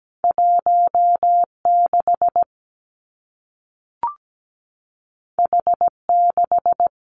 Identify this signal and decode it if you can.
Morse code, heard as a keyed tone, about 17 wpm, 700 Hz. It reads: 16 H6
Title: Two Challenging Letter-Number Combinations 17wpm